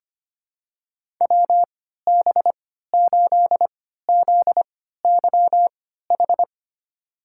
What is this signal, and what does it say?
Morse code, heard as a keyed tone, about 25 wpm, spaced as for 14 wpm, 690 Hz. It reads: WB8ZYH
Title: USA Call Signs - 6 Characters - 3x Character Spacing - 25wpm